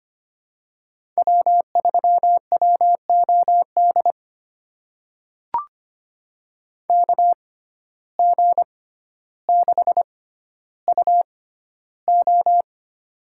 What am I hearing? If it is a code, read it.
Morse code, heard as a keyed tone, about 25 wpm, spaced as for 13 wpm, 700 Hz. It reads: W3WOD KG6UO
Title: USA Call Signs - 5 Characters - 6x Character Spacing - 25wpm